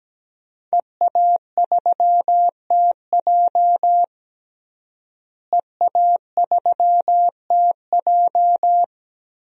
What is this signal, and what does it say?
Morse code, heard as a keyed tone, about 17 wpm, 700 Hz. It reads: EA3TJ EA3TJ
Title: DXCC-100 Call Signs - 5 Characters - Repeated 3x - 17wpm